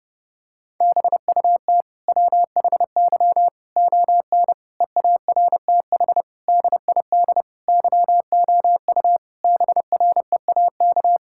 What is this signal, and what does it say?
Morse code, heard as a keyed tone, about 32 wpm, 700 Hz. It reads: BUT WHY ON EARTH DID YOU BREAK
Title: Sentences from Top 1000 Words - Review Part 6 - 30wpm